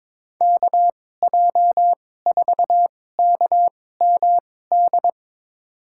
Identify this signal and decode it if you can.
Morse code, heard as a keyed tone, about 21 wpm, spaced as for 16 wpm, 700 Hz. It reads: KJ4KMD
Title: USA Call Signs - 6 Characters - 2x Character Spacing - 22wpm